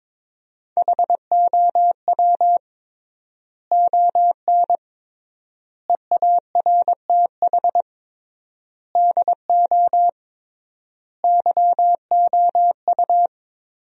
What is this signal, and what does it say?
Morse code, heard as a keyed tone, about 22 wpm, 700 Hz. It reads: HOW ON EARTH DO YOU
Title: 3x Word-Spacing - Sentences from Top 200 Words - 22wpm